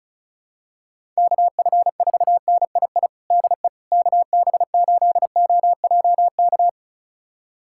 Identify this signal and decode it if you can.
Morse code, heard as a keyed tone, about 34 wpm, 700 Hz. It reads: KF4NII DE KB8OJK
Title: USA Callsign DE Callsign - 6 Characters - 35wpm